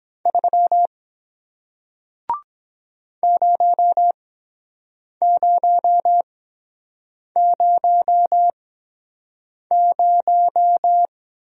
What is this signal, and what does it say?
Morse code, heard as a keyed tone, about 23 wpm, spaced as for 7 wpm, 700 Hz. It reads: 3 0000